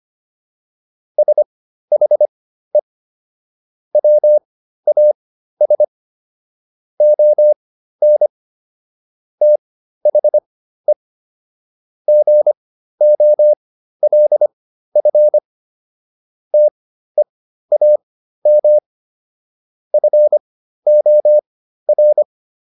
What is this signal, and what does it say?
Morse code, heard as a keyed tone, about 25 wpm, spaced as for 13 wpm, 600 Hz. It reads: SHE WAS ON THE GOLF TEAM FOR